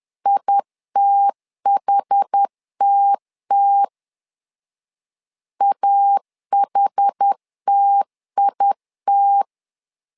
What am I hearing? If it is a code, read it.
Morse code, heard as a keyed tone, about 10 wpm, 800 Hz. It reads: ITHTT AHTIT
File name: LEZIONI / 2 UHDN / 2 UH 10wpm 1714